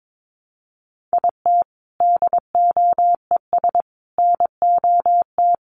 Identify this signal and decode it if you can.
Morse code, heard as a keyed tone, about 22 wpm, 700 Hz. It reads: IT DOES NOT
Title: Sentences from Top 800 Words 22wpm